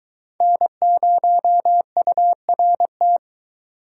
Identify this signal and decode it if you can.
Morse code, heard as a keyed tone, about 23 wpm, 700 Hz. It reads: N0URT